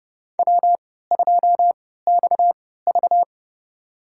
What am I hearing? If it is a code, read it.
Morse code, heard as a keyed tone, about 30 wpm, spaced as for 17 wpm, 700 Hz. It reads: W2XV
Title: USA Call Signs - 3x Character Spacing - 30wpm